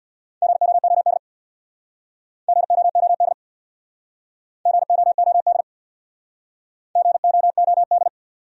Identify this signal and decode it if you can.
Morse code, heard as a keyed tone, about 63 wpm, 700 Hz. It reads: GOOD GOOD GOOD GOOD